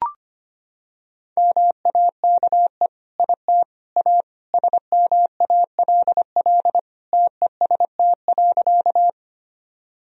Morse code, heard as a keyed tone, about 25 wpm, 700 Hz. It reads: MAKE IT A SMALL TEST.
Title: Sentences from Top 1000 Words - Review Part 3 - 25wpm